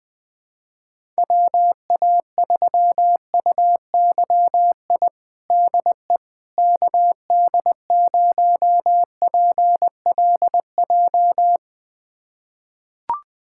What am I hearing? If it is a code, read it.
Morse code, heard as a keyed tone, about 21 wpm, 700 Hz. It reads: WA3UYI DE KD0PLJ
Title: USA Callsign DE Callsign - 6 Characters - 20wpm